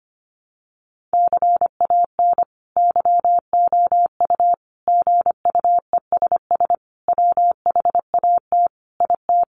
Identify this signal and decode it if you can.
Morse code, heard as a keyed tone, about 25 wpm, 700 Hz. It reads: CAN YOU GUESS WHAT IT
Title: Sentences from Top 1500 Words 25wpm